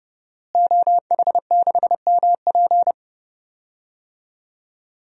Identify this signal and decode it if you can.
Morse code, heard as a keyed tone, about 30 wpm, 700 Hz. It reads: OH6MP
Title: DXCC-100 Call Signs - 5 Characters - Repeated 3x - 30wpm